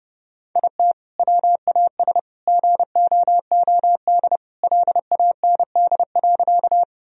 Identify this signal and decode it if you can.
Morse code, heard as a keyed tone, about 28 wpm, 700 Hz. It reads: IT WAS GOOD LAND.
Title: Sentences from Top 1000 Words - Review Part 1 - 30wpm